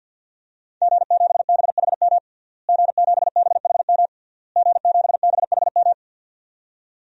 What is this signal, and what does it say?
Morse code, heard as a keyed tone, about 52 wpm, 700 Hz. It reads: G7BHM G7BHM G7BHM
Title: DXCC-100 Call Signs - 5 Characters - Repeated 3x - 50wpm